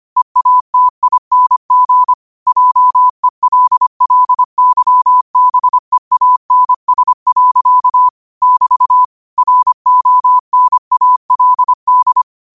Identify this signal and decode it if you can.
Morse code, heard as a keyed tone, about 25 wpm, 1000 Hz. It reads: EATING JELLYBEANS. <BT> RONALD
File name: qotdfiles / 25wpm